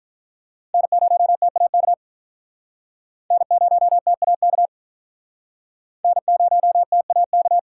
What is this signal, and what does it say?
Morse code, heard as a keyed tone, about 44 wpm, 700 Hz. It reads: N0TAK N0TAK N0TAK
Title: Speed Racing - USA 1x3 Call Signs 35wpm